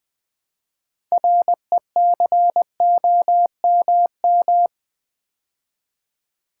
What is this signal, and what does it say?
Morse code, heard as a keyed tone, about 20 wpm, 700 Hz. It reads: RECOMM
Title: Build A Word - 3 Letters At A Time - 9-Letter Words - 20wpm